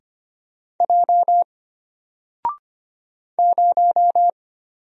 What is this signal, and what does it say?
Morse code, heard as a keyed tone, about 25 wpm, 700 Hz. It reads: J 0
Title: Single Letter-Number Alt 25wpm